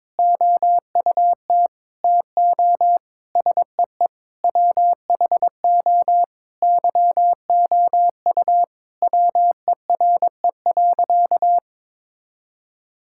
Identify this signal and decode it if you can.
Morse code, heard as a keyed tone, about 22 wpm, 700 Hz. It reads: OUT TO SEE WHO YOU WERE.
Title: Sentences from Top 1000 Words - Review Part 6 - 22wpm